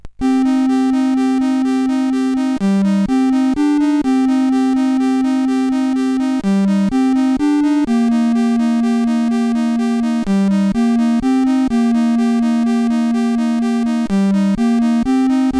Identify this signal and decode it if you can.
There is no code.